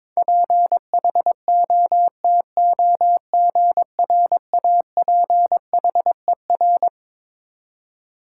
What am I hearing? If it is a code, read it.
Morse code, heard as a keyed tone, about 22 wpm, 700 Hz. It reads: PHOTOGRAPHER
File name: Twelve Letter Words - 22wpm